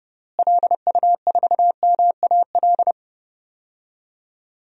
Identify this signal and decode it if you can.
Morse code, heard as a keyed tone, about 29 wpm, 700 Hz. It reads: LU4MAL
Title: DXCC-100 Call Signs - 6 Characters - Repeated 3x - 30wpm